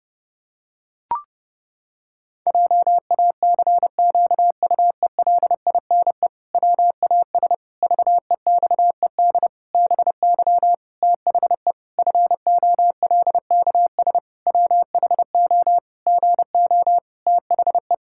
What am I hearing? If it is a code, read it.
Morse code, heard as a keyed tone, about 30 wpm, 700 Hz. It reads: JACQUELINE WAS VEXED BY THE FOLKS WHO GO THE
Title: Pangram Challenge v2 - 30wpm